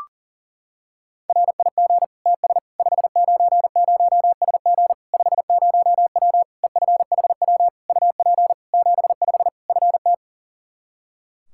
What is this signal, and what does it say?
Morse code, heard as a keyed tone, about 40 wpm, 700 Hz. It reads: RIG TS 590SG 50W EFHW UP 75 FT
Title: Practice QSO - Exchange 2 - Info - 40wpm